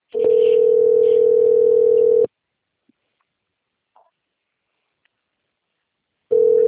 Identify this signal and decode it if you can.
Ringback tone (North America), 440 + 480 Hz.